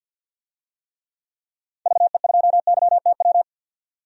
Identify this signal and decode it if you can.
Morse code, heard as a keyed tone, about 52 wpm, 700 Hz. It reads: VE2YTW
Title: DXCC-100 Call Signs - 6 Characters - 50wpm